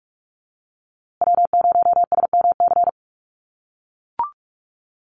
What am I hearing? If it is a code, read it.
Morse code, heard as a keyed tone, about 44 wpm, 700 Hz. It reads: W0SMC
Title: USA Call Signs Repeated 2x 45wpm